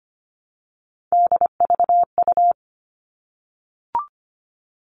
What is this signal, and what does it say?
Morse code, heard as a keyed tone, about 25 wpm, 700 Hz. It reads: DVU
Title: All Cut Numbers - 3-Digit 25wpm